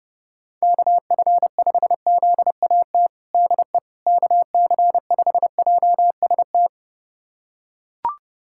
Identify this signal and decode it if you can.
Morse code, heard as a keyed tone, about 30 wpm, 700 Hz. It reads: KF5ZAT DE KC5JST
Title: USA Callsign DE Callsign - 6 Characters - 30wpm